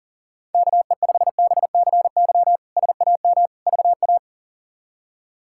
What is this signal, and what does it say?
Morse code, heard as a keyed tone, about 40 wpm, 700 Hz. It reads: KE5BCY SAM VA
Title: K1USN SST QSO Practice - 40wpm